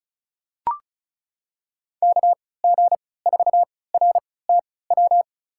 Morse code, heard as a keyed tone, about 36 wpm, spaced as for 20 wpm, 700 Hz. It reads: KG4RTW
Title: USA Call Signs - 3x Character Spacing - 35wpm